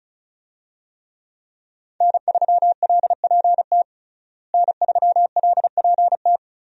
Morse code, heard as a keyed tone, about 34 wpm, 700 Hz. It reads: N3LPT N3LPT
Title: USA Call Signs - 5 Characters - Repeated 2x - 35wpm